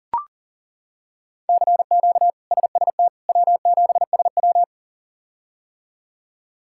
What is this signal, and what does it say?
Morse code, heard as a keyed tone, about 40 wpm, 700 Hz. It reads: CQ SST W7SW
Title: K1USN SST QSO Practice - 40wpm